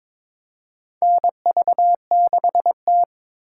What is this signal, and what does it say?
Morse code, heard as a keyed tone, about 22 wpm, 700 Hz. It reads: NV6T
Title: USA Call Signs - 4 Characters - 2x Character Spacing - 22wpm